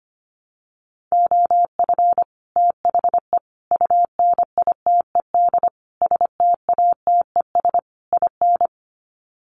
Morse code, heard as a keyed tone, about 25 wpm, 700 Hz. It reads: OF THE UNITED STATES IN
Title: Sets of 5 Words Encore 25wpm